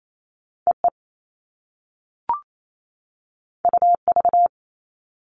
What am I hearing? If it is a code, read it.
Morse code, heard as a keyed tone, about 28 wpm, 700 Hz. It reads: EE UV
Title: All Cut Numbers - 2-Digit 28wpm